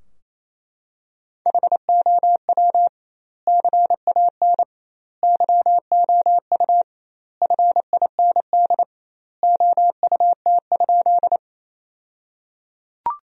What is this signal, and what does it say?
Morse code, heard as a keyed tone, about 28 wpm, 700 Hz. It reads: HOW CAN YOU FIND OUT?
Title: 2x Word-Spacing - Sentences from Top 200 Words - 28wpm